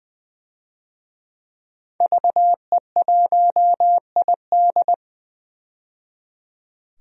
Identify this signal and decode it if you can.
Morse code, heard as a keyed tone, about 20 wpm, 700 Hz. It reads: VE1ID
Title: DXCC-100 Call Signs - 5 Characters - 20wpm